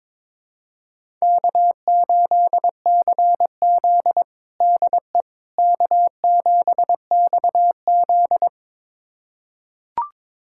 Morse code, heard as a keyed tone, about 22 wpm, 700 Hz. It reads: K8CZ DE K7XZ
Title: USA Callsign DE Callsign - 4 Characters - 22wpm